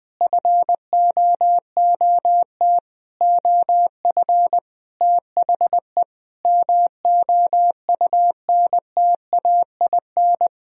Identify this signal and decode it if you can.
Morse code, heard as a keyed tone, about 20 wpm, 700 Hz. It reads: FOOT OF THE MOUNTAIN
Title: Sentences from Top 1000 Words - Review Part 3 - 20wpm